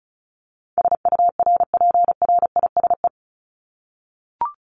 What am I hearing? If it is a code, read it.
Morse code, heard as a keyed tone, about 34 wpm, 700 Hz. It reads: SURPRISE
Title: Top 1751-2000 Words 35wpm